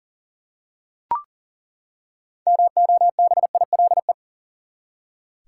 Morse code, heard as a keyed tone, about 40 wpm, 700 Hz. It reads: MOBILE
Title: Six Letter Words - 40wpm